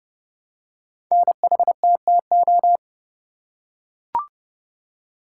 Morse code, heard as a keyed tone, about 30 wpm, 700 Hz. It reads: NHTTO
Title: Code Groups - 5 Letters - 30wpm